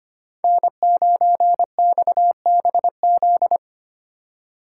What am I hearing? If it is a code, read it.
Morse code, heard as a keyed tone, about 25 wpm, 700 Hz. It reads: N9XBZ
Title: USA Call Signs - 3x Character Spacing - 25wpm